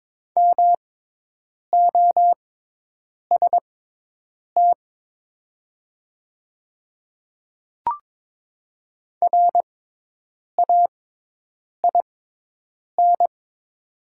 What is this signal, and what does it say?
Morse code, heard as a keyed tone, about 22 wpm, spaced as for 8 wpm, 700 Hz. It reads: MOST RAIN
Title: 6x Character Spacing - Top 500 Words - 22wpm